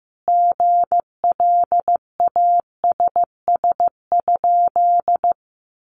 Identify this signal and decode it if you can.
Morse code, heard as a keyed tone, about 15 wpm, 700 Hz. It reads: GLASS?
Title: Sentences from Top 1300 Words 15wpm